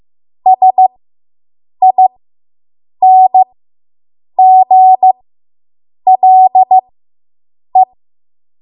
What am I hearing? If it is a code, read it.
Morse code, heard as a keyed tone, about 15 wpm, spaced as for 7 wpm, 750 Hz. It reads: SINGLE